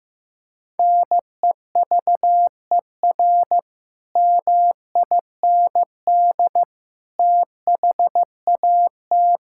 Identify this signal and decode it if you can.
Morse code, heard as a keyed tone, about 16 wpm, 700 Hz. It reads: NEVER MIND THAT
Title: Sentences from Top 1000 Words - Review Part 5 - 15wpm